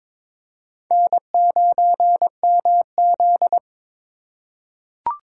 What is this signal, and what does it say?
Morse code, heard as a keyed tone, about 21 wpm, 700 Hz. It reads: N9MZ
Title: USA Call Signs - 3x Character Spacing - 22wpm